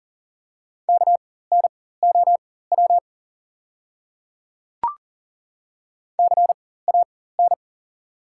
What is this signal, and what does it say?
Morse code, heard as a keyed tone, about 37 wpm, spaced as for 18 wpm, 700 Hz. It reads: KNOW CAN
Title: 4x Character Spacing - Top 100 Words - 40wpm